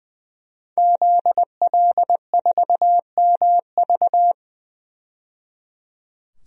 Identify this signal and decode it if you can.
Morse code, heard as a keyed tone, about 21 wpm, 700 Hz. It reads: ZL4MV
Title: DXCC-100 Call Signs - 5 Characters - 20wpm